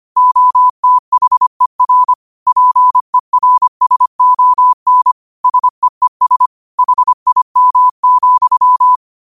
Morse code, heard as a keyed tone, about 25 wpm, 1000 Hz. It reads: OTHER PERSON SEES HIM,
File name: qotdfiles / 25wpm